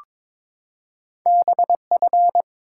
Morse code, heard as a keyed tone, about 23 wpm, 700 Hz. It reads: BF
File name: practice / 22wpm / Two Letter-Number Combinations - 22wpm